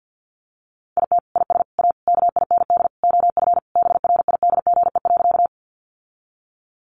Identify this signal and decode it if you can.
Morse code, heard as a keyed tone, about 50 wpm, 700 Hz. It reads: IT IS A KIND OF BRIDGE.
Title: Sentences from Top 1200 Words 50wpm